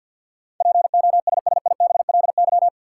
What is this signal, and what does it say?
Morse code, heard as a keyed tone, about 52 wpm, 700 Hz. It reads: POSSIBLY
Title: Sentences from Top 1300 Words - New Word Drill - 50wpm